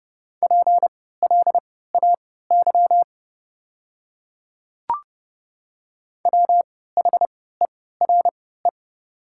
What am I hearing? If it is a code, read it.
Morse code, heard as a keyed tone, about 30 wpm, spaced as for 17 wpm, 700 Hz. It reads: PLAY WHERE
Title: 3x Character Spacing - Top 200 Words - 30wpm